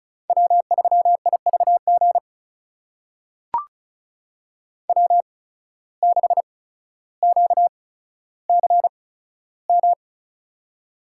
Morse code, heard as a keyed tone, about 36 wpm, spaced as for 15 wpm, 700 Hz. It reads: W3IVG W6QCM
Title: USA Call Signs - 8x Character Spacing - 35wpm